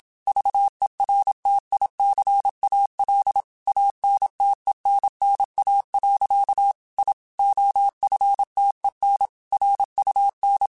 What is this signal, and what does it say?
Morse code, heard as a keyed tone, about 27 wpm, 800 Hz. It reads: VERTICAL ANTENNA. I OFTEN RUN
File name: QSOs / QSOs-mp3s / WA7QOX 2@25